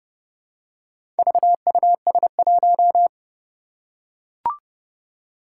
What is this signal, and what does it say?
Morse code, heard as a keyed tone, about 29 wpm, 700 Hz. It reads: VUS1
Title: Four Challenging Letter-Number Combinations 30wpm